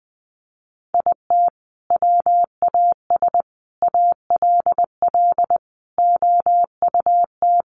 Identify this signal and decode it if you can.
Morse code, heard as a keyed tone, about 19 wpm, 700 Hz. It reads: IT WAS ALL OUT
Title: Sentences from Top 1200 Words 20wpm